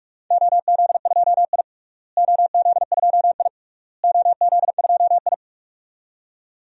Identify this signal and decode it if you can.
Morse code, heard as a keyed tone, about 41 wpm, 700 Hz. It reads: OZ2I OZ2I OZ2I
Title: DXCC-100 Call Signs - 4 Characters - Repeated 3x - 45wpm